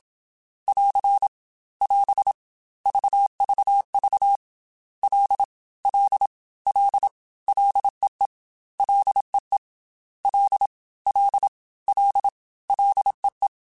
Morse code, heard as a keyed tone, about 27 wpm, 800 Hz. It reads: <AR> <AS> VVV L L L LEE LEE L L L LEE
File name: F and L / F and L mp3 / F and L Prefixes 3x2 Twice @25